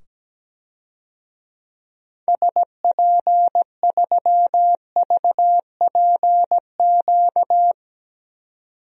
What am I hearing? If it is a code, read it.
Morse code, heard as a keyed tone, about 17 wpm, 700 Hz. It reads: SP3VPQ
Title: DXCC-100 Call Signs - 6 Characters - 17wpm